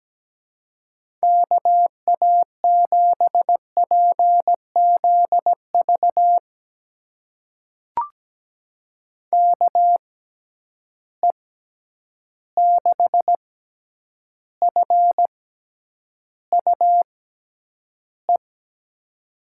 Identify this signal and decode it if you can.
Morse code, heard as a keyed tone, about 17 wpm, spaced as for 9 wpm, 700 Hz. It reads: KA7PZV KE6FUE
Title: USA Call Signs - 6 Characters - 6x Character Spacing - 17wpm